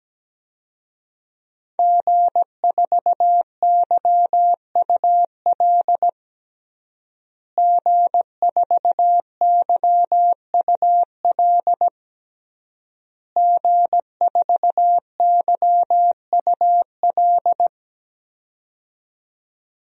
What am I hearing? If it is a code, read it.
Morse code, heard as a keyed tone, about 17 wpm, 700 Hz. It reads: G4YUL G4YUL G4YUL
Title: DXCC-100 Call Signs - 5 Characters - Repeated 3x - 17wpm